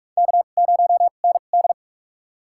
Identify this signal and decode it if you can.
Morse code, heard as a keyed tone, about 43 wpm, spaced as for 32 wpm, 700 Hz. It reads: K0ND
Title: USA Call Signs - 4 Characters - 2x Character Spacing - 45wpm